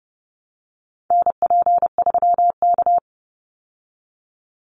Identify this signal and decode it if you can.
Morse code, heard as a keyed tone, about 30 wpm, 700 Hz. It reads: NP3K
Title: Call Signs Repeated 3x - 30wpm